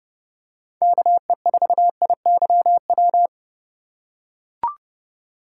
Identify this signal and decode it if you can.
Morse code, heard as a keyed tone, about 30 wpm, 700 Hz. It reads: KE4IYW
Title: USA Call Signs - 2x Character Spacing - 30wpm